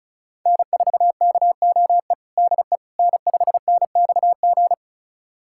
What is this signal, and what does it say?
Morse code, heard as a keyed tone, about 36 wpm, 700 Hz. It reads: N4KOE DE N5NXG
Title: USA Callsign DE Callsign - 5 Characters - 35wpm